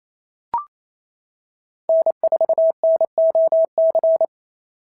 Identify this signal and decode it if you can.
Morse code, heard as a keyed tone, about 28 wpm, 650 Hz. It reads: N4NOC
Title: SOTA QSO Practice 28wpm